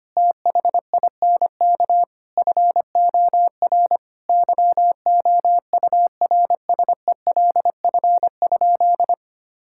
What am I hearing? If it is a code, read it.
Morse code, heard as a keyed tone, about 25 wpm, 700 Hz. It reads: THINK FOR YOURSELF?
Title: Sentences from Top 1000 Words - Review Part 6 - 25wpm